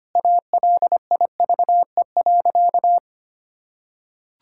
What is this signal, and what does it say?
Morse code, heard as a keyed tone, about 25 wpm, 700 Hz. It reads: ALIVE.
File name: practice / 25wpm / Sentences with 2-Word Phrases - 25wpm